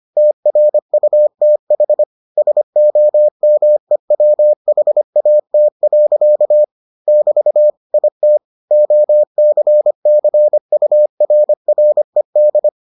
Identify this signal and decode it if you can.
Morse code, heard as a keyed tone, about 25 wpm, 600 Hz. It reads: TRUTH SOMEWHAT. <BT> IT OCCURRED